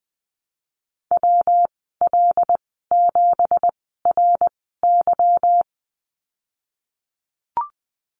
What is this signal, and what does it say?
Morse code, heard as a keyed tone, about 20 wpm, spaced as for 15 wpm, 700 Hz. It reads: WL7RY
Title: 186 Lesson 30g - Letter b - Call Signs 20wpm